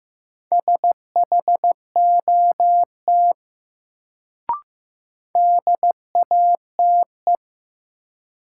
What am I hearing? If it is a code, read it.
Morse code, heard as a keyed tone, about 15 wpm, 700 Hz. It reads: SHOT DATE
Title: Sentences from Top 1100 Words - New Word Drill Alt - 15wpm